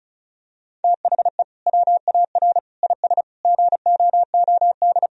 Morse code, heard as a keyed tone, about 36 wpm, 700 Hz. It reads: THE WAR IS GOOD
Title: Sentences from Top 1000 Words - Review Part 4 - 35wpm